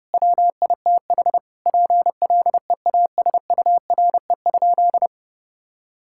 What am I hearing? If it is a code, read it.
Morse code, heard as a keyed tone, about 30 wpm, 700 Hz. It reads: WITH PLEASURE?
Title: Sentences from Top 1600 Words 30wpm